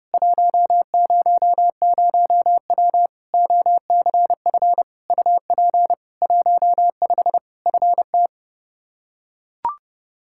Morse code, heard as a keyed tone, about 30 wpm, 700 Hz. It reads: 100W OCF UP 15 FT
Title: K1USN SST QSO Practice 30wpm